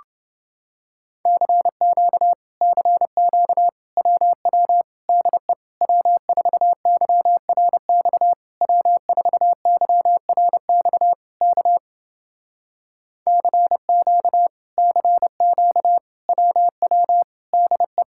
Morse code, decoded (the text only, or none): CQ CQ WW DE W4YRX W4YRX K CQ CQ WW DE